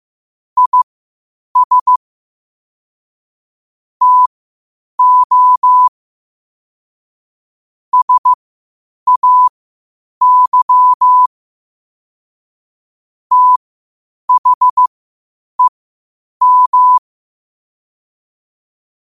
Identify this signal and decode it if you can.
Morse code, heard as a keyed tone, about 15 wpm, spaced as for 8 wpm, 1000 Hz. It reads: IS TO SAY THEM
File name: qotdfiles / 8wpm